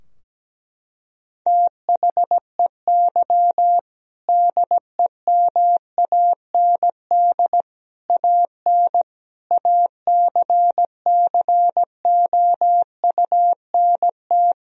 Morse code, heard as a keyed tone, about 17 wpm, 700 Hz. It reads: THEY DEMAND AN ACCOUNT
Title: Sentences from Top 1000 Words - Review Part 4 - 17wpm